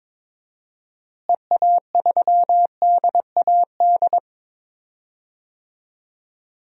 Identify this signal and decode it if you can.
Morse code, heard as a keyed tone, about 22 wpm, 700 Hz. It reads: EA3DAD
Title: DXCC-100 Call Signs - 6 Characters - Repeated 3x - 22wpm